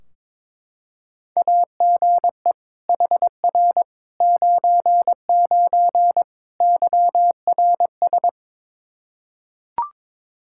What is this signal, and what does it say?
Morse code, heard as a keyed tone, about 22 wpm, 700 Hz. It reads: AGE HR 99 YRS
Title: Practice QSO - Exchange 3 - More Info - 22wpm